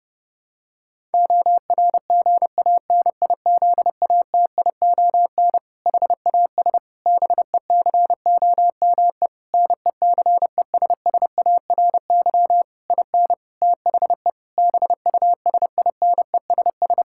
Morse code, heard as a keyed tone, about 30 wpm, 700 Hz. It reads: ORGANIZATION HAS BECOME NECESSARY IN THE BUSINESS
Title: Sentences from Top 1000 Words - Review Part 5 - 30wpm